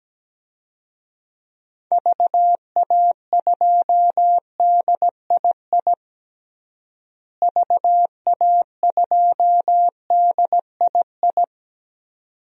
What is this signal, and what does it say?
Morse code, heard as a keyed tone, about 17 wpm, 700 Hz. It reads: VA2DII VA2DII
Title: DXCC-100 Call Signs - 6 Characters - Repeated 2x - 17wpm